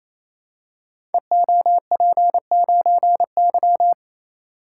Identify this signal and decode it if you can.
Morse code, heard as a keyed tone, about 29 wpm, 700 Hz. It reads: EOP9Y
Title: Code Groups - 5 Letter:Numbers - 28wpm